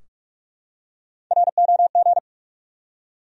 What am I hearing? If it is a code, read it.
Morse code, heard as a keyed tone, about 43 wpm, 700 Hz. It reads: ROG
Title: USA Common Names - Max 3 Letters - 45wpm